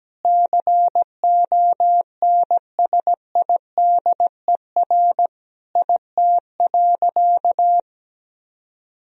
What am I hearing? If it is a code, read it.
Morse code, heard as a keyed tone, about 17 wpm, 700 Hz. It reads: CONSIDER IT.
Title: Sentences from Top 1000 Words - Review Part 4 - 17wpm